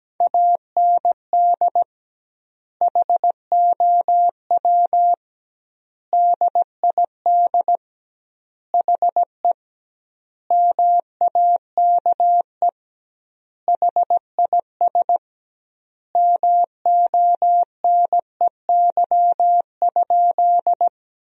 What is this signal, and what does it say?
Morse code, heard as a keyed tone, about 17 wpm, 700 Hz. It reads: AND HOW DID HE MAKE HIS MONEY?
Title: 2x Word-Spacing - Sentences from Top 400 Words - 17wpm